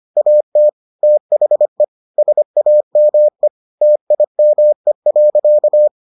Morse code, heard as a keyed tone, about 25 wpm, 600 Hz. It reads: AT THE SAME TIME.